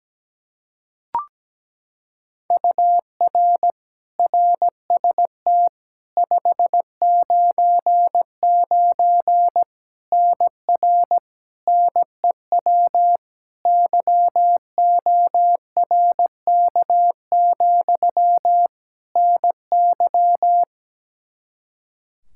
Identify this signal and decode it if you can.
Morse code, heard as a keyed tone, about 17 wpm, 700 Hz. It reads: UR RST 599 NR NEW YORK, NY
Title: K1USN SST QSO Practice 17wpm